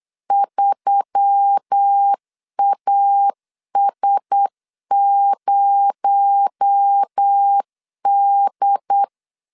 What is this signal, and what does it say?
Morse code, heard as a keyed tone, about 8 wpm, 800 Hz. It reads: 3AS0D
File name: CW / DET / 1 6 1500C S C 8wpm 5219